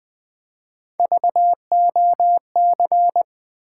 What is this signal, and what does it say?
Morse code, heard as a keyed tone, about 20 wpm, 700 Hz. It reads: VOC
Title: Build A Word - 1 Letter At A Time - 5-Letter Words - 20wpm